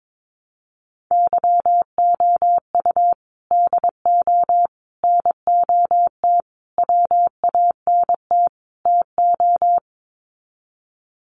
Morse code, heard as a keyed tone, about 22 wpm, 700 Hz. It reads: YOU DO NOT WANT TO
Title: Sets of 5 Words 22wpm